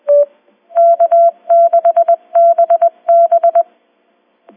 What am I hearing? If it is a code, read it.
Morse code, heard as a keyed tone, about 20 wpm, 680 Hz. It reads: K6BB